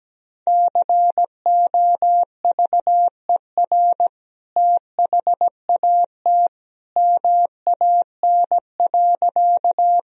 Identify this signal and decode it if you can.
Morse code, heard as a keyed tone, about 17 wpm, 700 Hz. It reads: COVER THAT MAN.
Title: Sentences from Top 1000 Words - Review Part 1 - 17wpm